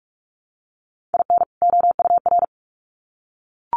Morse code, heard as a keyed tone, about 44 wpm, 700 Hz. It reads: IN OUR
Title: Sets of 2 Words 45wpm